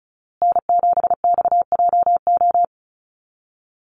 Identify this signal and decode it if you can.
Morse code, heard as a keyed tone, about 34 wpm, 700 Hz. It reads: N7XJO
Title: USA Call Signs Repeated 2x 35wpm